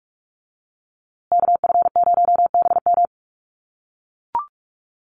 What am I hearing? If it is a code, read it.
Morse code, heard as a keyed tone, about 44 wpm, 700 Hz. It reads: KF0BM
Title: USA Call Signs Repeated 2x 45wpm